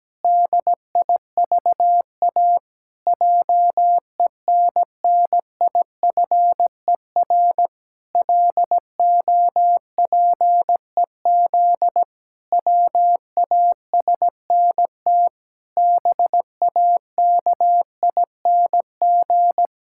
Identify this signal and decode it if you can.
Morse code, heard as a keyed tone, about 17 wpm, 700 Hz. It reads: DIVA JENNIFER LOPEZ WASNT BAKING
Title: Pangram Challenge v2 - 17wpm